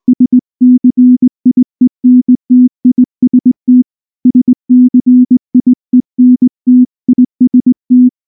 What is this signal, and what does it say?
Morse code, heard as a keyed tone, about 19 wpm, 250 Hz. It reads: SCIENTIST SCIENTIST
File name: morse / morse-training-20WPM-3